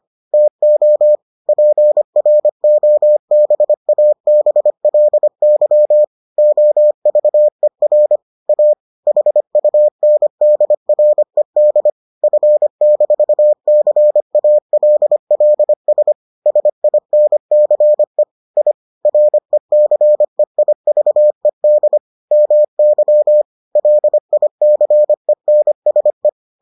Morse code, heard as a keyed tone, about 25 wpm, 600 Hz. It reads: TO PROBABLY OVER A HUNDRED F-CALLS SINCE I RECEIVED MY LICENSE